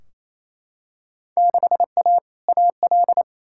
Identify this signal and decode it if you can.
Morse code, heard as a keyed tone, about 28 wpm, 700 Hz. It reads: 6A AL
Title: Field Day Exchanges Encore - 28wpm